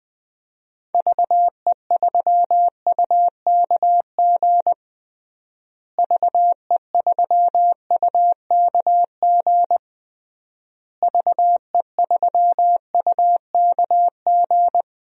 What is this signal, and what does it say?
Morse code, heard as a keyed tone, about 20 wpm, 700 Hz. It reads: VE3UKG VE3UKG VE3UKG
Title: DXCC-100 Call Signs - 6 Characters - Repeated 3x - 20wpm